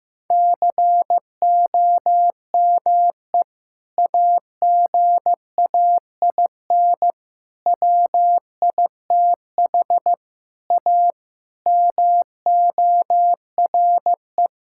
Morse code, heard as a keyed tone, about 16 wpm, 700 Hz. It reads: COME AGAIN WITH A MORE
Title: Sentences from Top 1000 Words - Review Part 6 - 15wpm